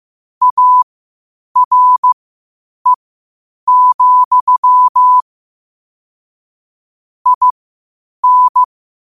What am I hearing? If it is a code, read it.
Morse code, heard as a keyed tone, about 15 wpm, spaced as for 8 wpm, 1000 Hz. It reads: ARE, IN
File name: qotdfiles / 8wpm